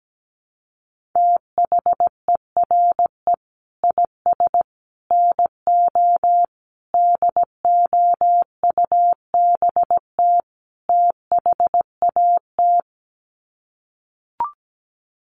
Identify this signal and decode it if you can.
Morse code, heard as a keyed tone, about 17 wpm, 700 Hz. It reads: THERE IS NO DOUBT THAT
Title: Sets of 5 Words 17wpm